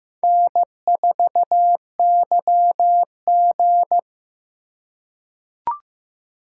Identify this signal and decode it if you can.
Morse code, heard as a keyed tone, about 15 wpm, 700 Hz. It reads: N4YG
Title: USA Call Signs - 3x Character Spacing - 15wpm